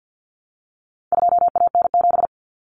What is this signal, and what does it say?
Morse code, heard as a keyed tone, about 50 wpm, 700 Hz. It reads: 2AN7
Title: Common Cut Numbers - 4-Digit 50wpm